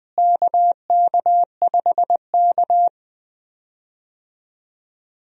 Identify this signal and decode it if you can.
Morse code, heard as a keyed tone, about 20 wpm, 700 Hz. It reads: KK5K
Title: USA Call Signs - 4 Characters - Repeated 2x - 20wpm